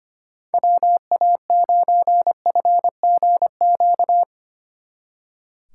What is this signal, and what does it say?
Morse code, heard as a keyed tone, about 30 wpm, 700 Hz. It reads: WA9FGQ
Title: Speed Racing - USA 2x3 Call Signs 25wpm